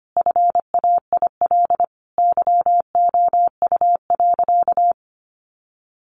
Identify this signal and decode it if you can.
Morse code, heard as a keyed tone, about 25 wpm, 700 Hz. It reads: FAIL YOU.
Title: Sentences from Top 1100 Words 25wpm